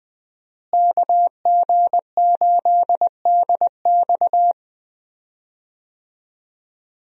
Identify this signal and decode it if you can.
Morse code, heard as a keyed tone, about 21 wpm, 700 Hz. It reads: KG8DX
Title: USA Call Signs - 5 Characters - Repeated 2x - 20wpm